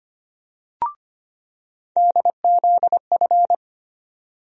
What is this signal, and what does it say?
Morse code, heard as a keyed tone, about 25 wpm, 700 Hz. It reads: DZF
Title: USA Call Sign Postfixes - 3 Letters 25wpm